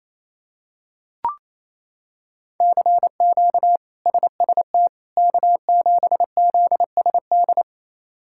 Morse code, heard as a keyed tone, about 28 wpm, 700 Hz. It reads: CQ SST K7ZSD
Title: K1USN SST QSO Practice - 28wpm